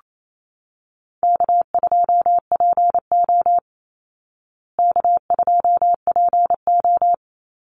Morse code, heard as a keyed tone, about 28 wpm, 700 Hz. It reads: K2PO K2PO
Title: Easy Call Signs Repeated 2x - 28wpm